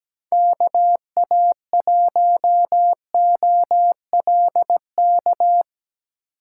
Morse code, heard as a keyed tone, about 19 wpm, 700 Hz. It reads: KA1OLK
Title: Speed Racing - USA 2x3 Call Signs 17wpm